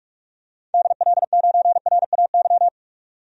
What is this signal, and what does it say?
Morse code, heard as a keyed tone, about 42 wpm, 700 Hz. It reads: DL9RAY
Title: DXCC-100 Call Signs - 6 Characters - 45wpm